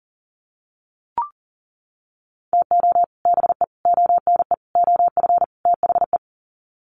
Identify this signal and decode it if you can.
Morse code, heard as a keyed tone, about 38 wpm, 700 Hz. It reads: TO BE ONE OF THE
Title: Sets of 5 Words 40wpm